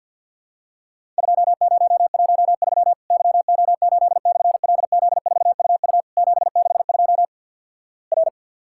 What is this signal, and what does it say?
Morse code, heard as a keyed tone, about 50 wpm, 700 Hz. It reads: 2013 YO8/LZ4UU 762 R
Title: Worked All Europe DX Contest - QTC Exchange - 50wpm